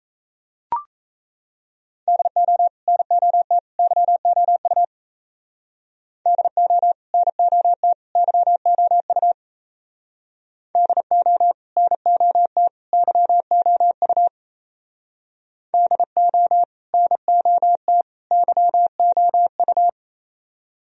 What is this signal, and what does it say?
Morse code, heard as a keyed tone, about 34 wpm, 700 Hz. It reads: DO NOT YOU DO NOT YOU DO NOT YOU DO NOT YOU
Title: Speed-Racing - Sets of 3 Words 28wpm